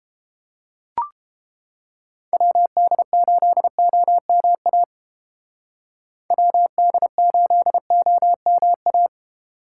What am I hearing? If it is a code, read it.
Morse code, heard as a keyed tone, about 30 wpm, 700 Hz. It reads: WD8OMA WD8OMA